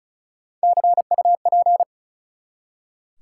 Morse code, heard as a keyed tone, about 36 wpm, 700 Hz. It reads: CUP
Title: Three Letter Words - 35wpm